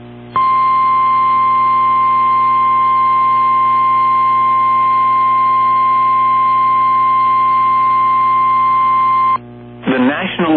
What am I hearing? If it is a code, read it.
Emergency Alert System (SAME) tones (the 1050 Hz attention tone), with speech. It is heard from 0.3 to 9.4 s.
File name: eas / KWBE / KWBE